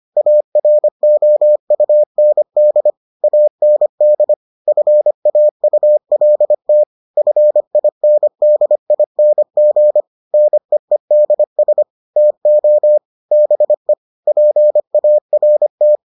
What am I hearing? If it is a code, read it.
Morse code, heard as a keyed tone, about 25 wpm, 600 Hz. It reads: AROUND AND FAULT FINDING NEEDS TO BE PART